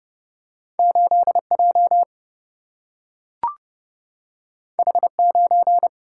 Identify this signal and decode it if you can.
Morse code, heard as a keyed tone, about 30 wpm, 700 Hz. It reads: 8 J H 9
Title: Single-Letter Single-Number 30wpm